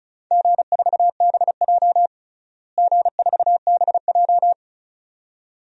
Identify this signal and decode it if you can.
Morse code, heard as a keyed tone, about 36 wpm, 700 Hz. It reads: G4BJ G4BJ
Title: DXCC-100 Call Signs - 4 Characters - Repeated 2x - 35wpm